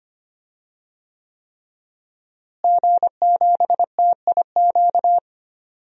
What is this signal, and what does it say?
Morse code, heard as a keyed tone, about 25 wpm, 700 Hz. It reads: G7TIQ
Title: DXCC-100 Call Signs - 5 Characters - Repeated 3x - 25wpm